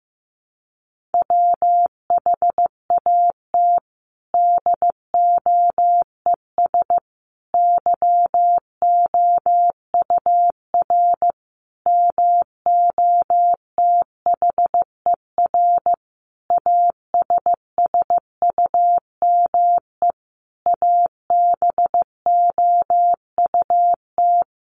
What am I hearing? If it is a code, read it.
Morse code, heard as a keyed tone, about 15 wpm, 700 Hz. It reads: WHAT DOES YOUR MOTHER ASSUME ABOUT
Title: Sentences from Top 1500 Words 15wpm